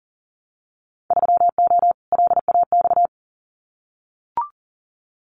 Morse code, heard as a keyed tone, about 39 wpm, 700 Hz. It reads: 3O LAX
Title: Winter Field Day Exchanges Encore 40wpm